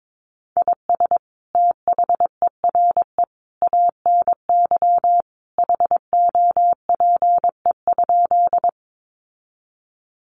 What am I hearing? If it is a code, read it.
Morse code, heard as a keyed tone, about 22 wpm, 700 Hz. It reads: IS THERE ANY HOPE?
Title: Sentences from Top 1000 Words 22wpm